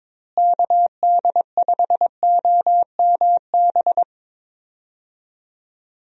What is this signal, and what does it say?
Morse code, heard as a keyed tone, about 21 wpm, 700 Hz. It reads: KD5OMB
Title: USA Call Signs - 6 Characters - Repeated 3x - 22wpm